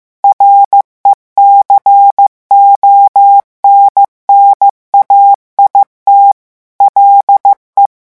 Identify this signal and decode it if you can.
Morse code, heard as a keyed tone, about 15 wpm, 800 Hz. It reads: RECONNAIT LE
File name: morse mp3 / revision9 citations humoristiques vitesse 14 mots minute